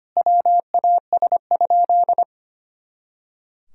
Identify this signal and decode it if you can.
Morse code, heard as a keyed tone, about 25 wpm, 700 Hz. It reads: WAS?
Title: Sentences from Top 200 Words and Abbreviations 25wpm